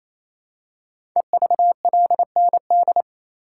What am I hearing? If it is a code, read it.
Morse code, heard as a keyed tone, about 27 wpm, 700 Hz. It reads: EVLND
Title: Code Groups - 5 Letter:Numbers - 28wpm